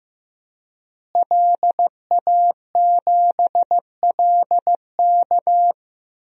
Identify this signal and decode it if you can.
Morse code, heard as a keyed tone, about 15 wpm, 700 Hz. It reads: LA7LK
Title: DXCC-100 Call Signs - 5 Characters - Repeated 2x - 15wpm